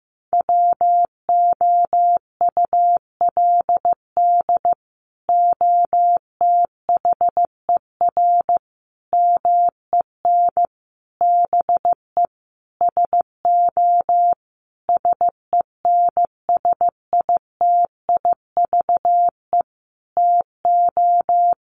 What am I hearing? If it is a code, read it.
Morse code, heard as a keyed tone, about 15 wpm, 700 Hz. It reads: WOULD OTHER MEN BE SO SENSITIVE TO
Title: Sentences from Top 1400 Words 15wpm